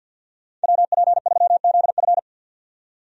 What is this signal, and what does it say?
Morse code, heard as a keyed tone, about 50 wpm, 700 Hz. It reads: WP3ZF
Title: USA Call Signs - 5 Characters - 50wpm